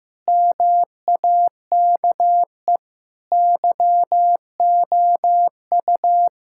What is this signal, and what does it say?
Morse code, heard as a keyed tone, about 15 wpm, 700 Hz. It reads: MAKE YOU
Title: Sentences from Top 1000 Words - Review Part 5 - 15wpm